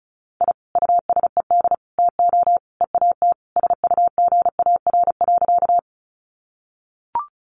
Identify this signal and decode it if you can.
Morse code, heard as a keyed tone, about 34 wpm, 700 Hz. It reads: I USED TO EAT SUGAR.
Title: Sentences from Top 1200 Words 35wpm